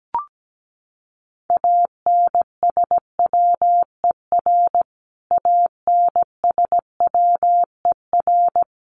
Morse code, heard as a keyed tone, about 17 wpm, 700 Hz. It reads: ANSWER ANSWER
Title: Top 300 Words Repeated 17wpm